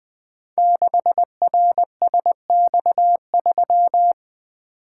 Morse code, heard as a keyed tone, about 20 wpm, 700 Hz. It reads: 6RSX3
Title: Code Groups - 5 Letter:Numbers - 20wpm